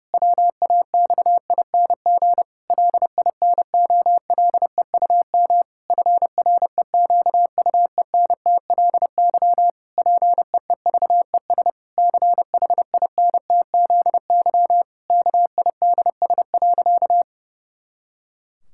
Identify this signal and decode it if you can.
Morse code, heard as a keyed tone, about 30 wpm, 700 Hz. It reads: WAXING LINOLEUM FREQUENTLY PEEVES CHINTZY KIDS.
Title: Pangram Challenge v2 - 30wpm